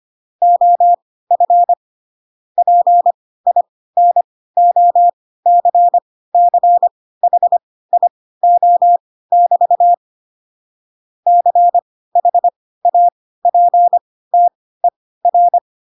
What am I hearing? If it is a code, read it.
Morse code, heard as a keyed tone, about 25 wpm, spaced as for 16 wpm, 700 Hz. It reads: OF PINOCCHIO<BT> CHAPTER
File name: Pinocchio / Pinocchio ch5 10 25 ik1hge